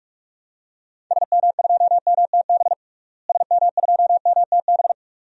Morse code, heard as a keyed tone, about 44 wpm, 700 Hz. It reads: SM2MTB SM2MTB
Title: DXCC-100 Call Signs - 6 Characters - Repeated 2x - 45wpm